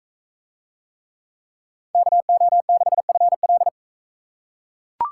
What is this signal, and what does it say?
Morse code, heard as a keyed tone, about 40 wpm, 700 Hz. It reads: KO6FL